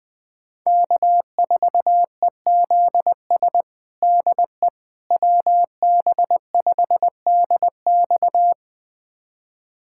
Morse code, heard as a keyed tone, about 20 wpm, 700 Hz. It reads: K4EZS DE WB5DX
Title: USA Callsign DE Callsign - 5 Characters - 20wpm